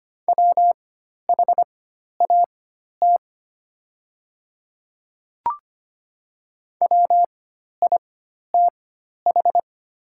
Morse code, heard as a keyed tone, about 25 wpm, spaced as for 12 wpm, 700 Hz. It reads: WHAT WITH
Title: 4x Character Spacing - Top 100 Words - 25wpm